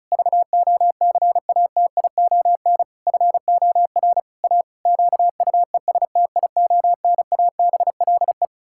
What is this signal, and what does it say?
Morse code, heard as a keyed tone, about 36 wpm, 700 Hz. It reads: VOCATION FOR A QUESTIONABLE
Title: Pangram Challenge v2 - 35wpm